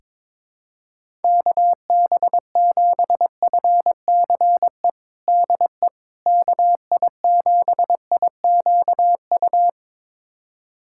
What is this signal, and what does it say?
Morse code, heard as a keyed tone, about 22 wpm, 700 Hz. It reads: KB7FCE DE KI7IQU
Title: USA Callsign DE Callsign - 6 Characters - 22wpm